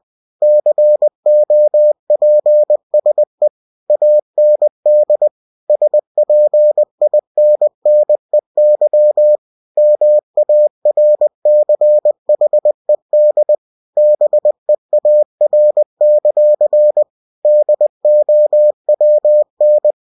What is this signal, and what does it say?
Morse code, heard as a keyed tone, about 20 wpm, 600 Hz. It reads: COPSE AND SPINNEY MARCHED BEAR; DOWN
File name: cw-books / winnie-the-pooh / winnie-20wpm-page-0028